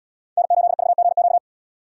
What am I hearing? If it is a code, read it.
Morse code, heard as a keyed tone, about 71 wpm, 700 Hz. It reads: N8VGQ